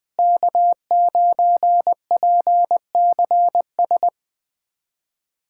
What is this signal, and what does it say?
Morse code, heard as a keyed tone, about 21 wpm, 700 Hz. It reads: K9PCS
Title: USA Callsign DE Callsign - 5 Characters - 20wpm